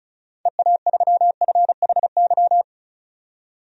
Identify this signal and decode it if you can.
Morse code, heard as a keyed tone, about 36 wpm, 700 Hz. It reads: EA3FHY
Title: DXCC-100 Call Signs - 6 Characters - 35wpm